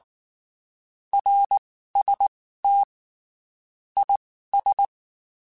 Morse code, heard as a keyed tone, about 19 wpm, spaced as for 14 wpm, 800 Hz. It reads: RST IS
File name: (spaced) / QSOs Spaced-mp3s / WA7QOX 2 @18 sp